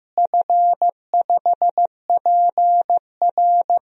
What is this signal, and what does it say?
Morse code, heard as a keyed tone, about 15 wpm, 700 Hz. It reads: F5PR
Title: DXCC-100 Call Signs - 4 Characters - Repeated 3x - 15wpm